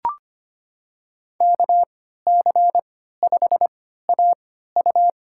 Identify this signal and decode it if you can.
Morse code, heard as a keyed tone, about 25 wpm, spaced as for 14 wpm, 700 Hz. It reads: KC5AU
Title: USA Call Signs - 3x Character Spacing - 25wpm